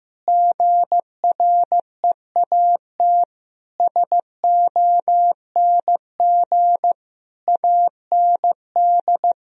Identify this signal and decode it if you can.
Morse code, heard as a keyed tone, about 15 wpm, 700 Hz. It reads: GREAT SONG AND
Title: Sentences from Top 1000 Words - Review Part 4 - 15wpm